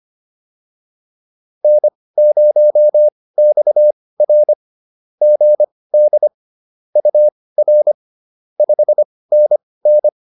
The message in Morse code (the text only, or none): N0XR GD UR 5NN